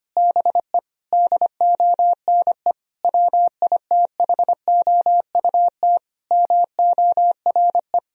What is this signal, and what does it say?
Morse code, heard as a keyed tone, about 25 wpm, 700 Hz. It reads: BE DONE WITHOUT MORE
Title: Sentences from Top 1000 Words - Review Part 3 - 25wpm